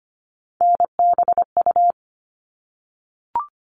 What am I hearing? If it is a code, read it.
Morse code, heard as a keyed tone, about 25 wpm, 700 Hz. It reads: NBU
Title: All Cut Numbers - 3-Digit 25wpm